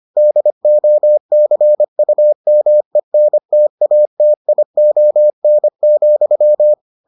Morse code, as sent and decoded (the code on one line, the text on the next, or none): -.. --- -.-. ..- -- . -. - .- - .. --- -. --..--
DOCUMENTATION,